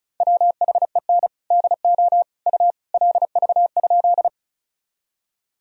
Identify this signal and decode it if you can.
Morse code, heard as a keyed tone, about 34 wpm, 700 Hz. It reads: WHEN DO U LV?
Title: Sentences from Top 400 Words and Abbreviations 35wpm